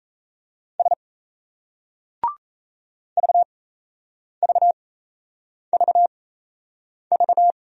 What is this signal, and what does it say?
Morse code, heard as a keyed tone, about 36 wpm, spaced as for 8 wpm, 700 Hz. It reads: S VVVV